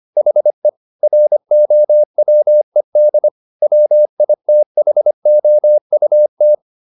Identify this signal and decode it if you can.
Morse code, heard as a keyed tone, about 25 wpm, 600 Hz. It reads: HE ROWED WITHOUT
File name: the old man and the sea part 1 25wpm